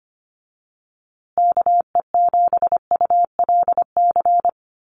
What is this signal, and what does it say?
Morse code, heard as a keyed tone, about 25 wpm, 700 Hz. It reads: KE7ULC
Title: USA Call Signs Repeated 2x 25wpm